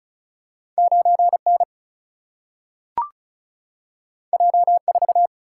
Morse code, heard as a keyed tone, about 36 wpm, 700 Hz. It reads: 9N J4
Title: Single-Letter Single-Number 35wpm